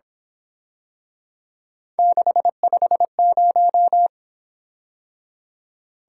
Morse code, heard as a keyed tone, about 26 wpm, 700 Hz. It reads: 650
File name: Speed-Racing - Three Letter-Number Combinations - 22wpm